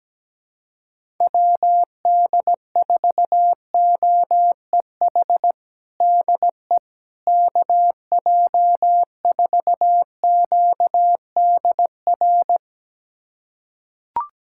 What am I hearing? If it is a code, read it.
Morse code, heard as a keyed tone, about 17 wpm, 700 Hz. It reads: WD4OEH DE KJ4QDR
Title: USA Callsign DE Callsign - 6 Characters - 17wpm